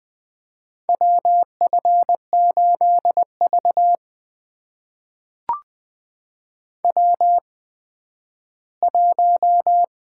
Morse code, heard as a keyed tone, about 20 wpm, spaced as for 12 wpm, 700 Hz. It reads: WF8V W1
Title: USA Call Signs - 4 Characters - 8x Character Spacing - 20wpm